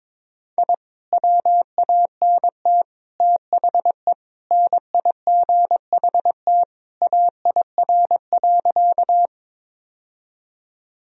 Morse code, heard as a keyed tone, about 22 wpm, 700 Hz. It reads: I WANT THE NIGHT AIR.
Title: Sentences from Top 1000 Words - Review Part 1 - 22wpm